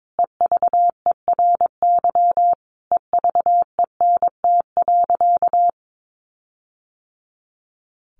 Morse code, heard as a keyed tone, about 22 wpm, 700 Hz. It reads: EVERY EVENT.
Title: Sentences from Top 1400 Words 22wpm